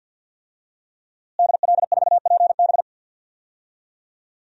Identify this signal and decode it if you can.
Morse code, heard as a keyed tone, about 45 wpm, 700 Hz. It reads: DL4PB
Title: DXCC-100 Call Signs - 5 Characters - Repeated 2x - 50wpm